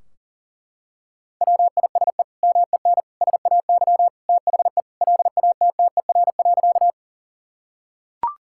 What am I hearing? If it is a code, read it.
Morse code, heard as a keyed tone, about 37 wpm, 700 Hz. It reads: WISE MEN SAY THE LATTER.
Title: Sentences from Top 1600 Words 40wpm